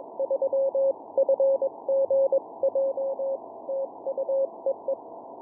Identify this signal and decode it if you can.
Morse code, heard as a keyed tone, about 21 wpm, 540 Hz. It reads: 3FGJ TU EE
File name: ja3fgj-g8bcg